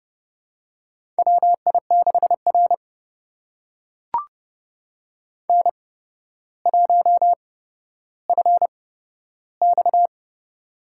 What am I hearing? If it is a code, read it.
Morse code, heard as a keyed tone, about 30 wpm, spaced as for 13 wpm, 700 Hz. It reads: WI6R N1FX
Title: USA Call Signs - 4 Characters - 8x Character Spacing - 30wpm